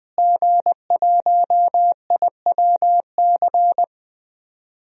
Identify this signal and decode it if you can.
Morse code, heard as a keyed tone, about 20 wpm, 700 Hz. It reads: G1IWC
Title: DXCC-100 Call Signs - 5 Characters - Repeated 3x - 20wpm